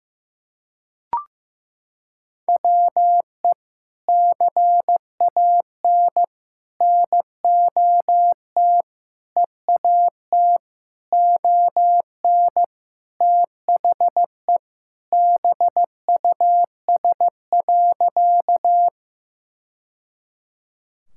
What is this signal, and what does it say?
Morse code, heard as a keyed tone, about 14 wpm, 700 Hz. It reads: WE CAN NOT EAT ON THE BUS.
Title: Sentences from Top 1000 Words - Review Part 6 - 15wpm